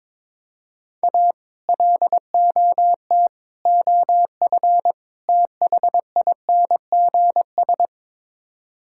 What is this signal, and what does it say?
Morse code, heard as a keyed tone, about 22 wpm, 700 Hz. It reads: A LOT OF THINGS
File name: practice / 22wpm / Build A Phrase - 5-Word Phrases - 22wpm